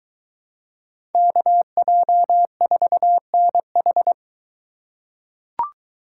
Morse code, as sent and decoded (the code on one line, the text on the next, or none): -.- .--- ....- -. ....
KJ4NH